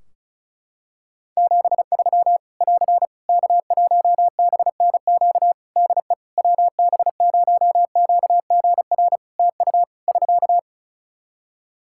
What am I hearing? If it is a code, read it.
Morse code, heard as a keyed tone, about 36 wpm, 700 Hz. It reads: 73 <AR> K1BNQ DE WB0QGR TU <SK>
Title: Practice QSO - Exchange 4 - Ending - 35wpm